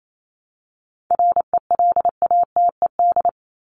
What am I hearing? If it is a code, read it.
Morse code, heard as a keyed tone, about 28 wpm, 700 Hz. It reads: RELATED
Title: ED Suffix - Top 100 Words with Base Words 28wpm